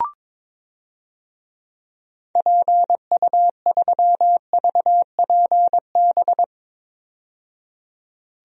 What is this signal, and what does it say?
Morse code, heard as a keyed tone, about 23 wpm, 700 Hz. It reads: PU3VPB
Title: DXCC-100 Call Signs - 6 Characters - 22wpm